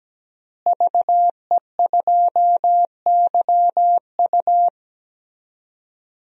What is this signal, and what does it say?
Morse code, heard as a keyed tone, about 17 wpm, 700 Hz. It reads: VE2YU
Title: DXCC-100 Call Signs - 5 Characters - Repeated 2x - 17wpm